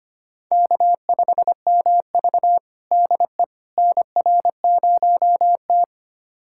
Morse code, heard as a keyed tone, about 25 wpm, 700 Hz. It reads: K5MV DE NR0T
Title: USA Callsign DE Callsign - 4 Characters - 25wpm